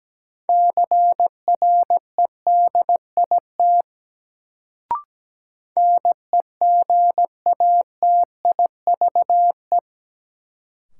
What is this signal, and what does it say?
Morse code, heard as a keyed tone, about 17 wpm, 700 Hz. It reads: CREDIT NEGATIVE
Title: Sentences from Top 1000 Words - New Word Drill - 17wpm